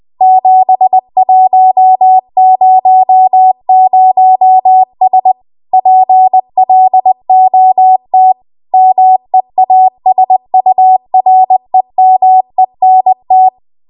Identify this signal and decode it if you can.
Morse code, heard as a keyed tone, about 20 wpm, 750 Hz. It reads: 7100S PLOT MEASUREMENT
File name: Archive / 20 WPM / 20WPM